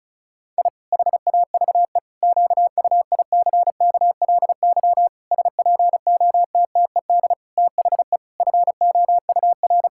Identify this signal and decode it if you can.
Morse code, heard as a keyed tone, about 36 wpm, 700 Hz. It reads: I HAVE QUICKLY SPOTTED THE FOUR
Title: Pangram Challenge v2 - 35wpm